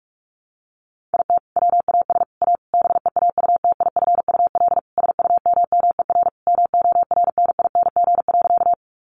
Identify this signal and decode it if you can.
Morse code, heard as a keyed tone, about 44 wpm, 700 Hz. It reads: IT WAS A BEAUTIFUL SUMMER MORNING.
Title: Sentences from Top 1100 Words 45wpm